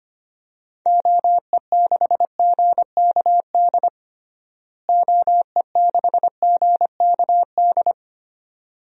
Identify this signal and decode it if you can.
Morse code, heard as a keyed tone, about 25 wpm, 700 Hz. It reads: OE6GKD OE6GKD
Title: DXCC-100 Call Signs - 6 Characters - Repeated 2x - 25wpm